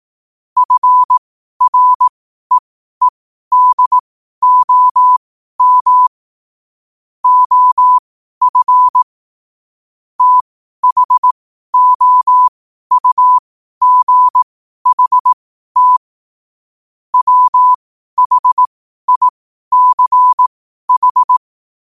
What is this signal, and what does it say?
Morse code, heard as a keyed tone, about 18 wpm, spaced as for 12 wpm, 1000 Hz. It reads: FREEDOM OF THOUGHT WHICH
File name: qotdfiles / 12wpm